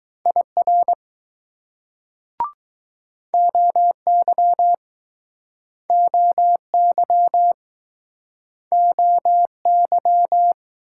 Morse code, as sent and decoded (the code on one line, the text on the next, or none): .. .-. / --- -.-- / --- -.-- / --- -.--
IR OY OY OY